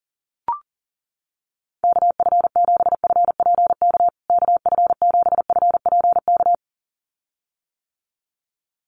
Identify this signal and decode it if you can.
Morse code, heard as a keyed tone, about 38 wpm, 700 Hz. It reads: KF7FPK KF7FPK
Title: USA Call Signs Repeated 2x 40wpm